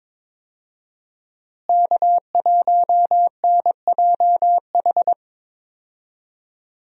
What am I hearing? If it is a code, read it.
Morse code, heard as a keyed tone, about 22 wpm, 700 Hz. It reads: K1NJH
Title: USA Call Signs - 5 Characters - Repeated 3x - 22wpm